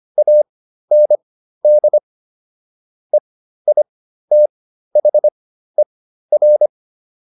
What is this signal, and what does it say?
Morse code, heard as a keyed tone, about 25 wpm, spaced as for 13 wpm, 600 Hz. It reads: AND EITHER